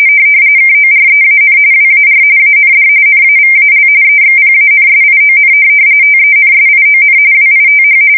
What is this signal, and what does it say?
RTTY, about 45.45 baud. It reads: KOREAN WAR AND NAVAL EXPANSION TENSION WITH
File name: fsk170 / usnavy5